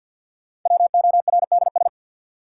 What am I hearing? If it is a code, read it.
Morse code, heard as a keyed tone, about 52 wpm, 700 Hz. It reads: WORDS
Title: Sentences from Top 1000 Words - Review Part 6 - New Word Drill - 50wpm